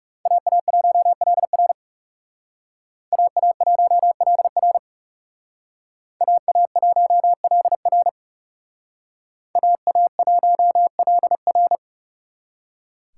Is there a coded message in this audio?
Yes: Morse code from 0.3 to 11.8 s, heard as a keyed tone, about 39 wpm, 700 Hz. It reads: AA1LR AA1LR AA1LR AA1LR